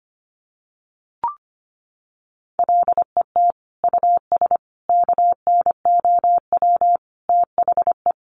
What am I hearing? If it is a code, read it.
Morse code, heard as a keyed tone, about 25 wpm, 700 Hz. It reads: LET US KNOW THE
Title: Sentences from Top 1200 Words 25wpm